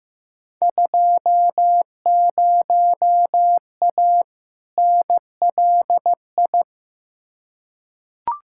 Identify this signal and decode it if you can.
Morse code, heard as a keyed tone, about 15 wpm, 700 Hz. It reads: 20A NLI
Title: Field Day Exchanges Encore - 15wpm